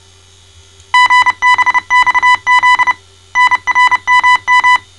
Morse code, heard as a keyed tone, about 29 wpm, 1000 Hz. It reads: GBXZ NRMM
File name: mrx / malta